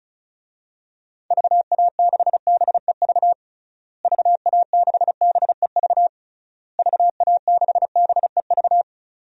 Morse code, heard as a keyed tone, about 34 wpm, 700 Hz. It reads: VA6BEV VA6BEV VA6BEV
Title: DXCC-100 Call Signs - 6 Characters - Repeated 3x - 35wpm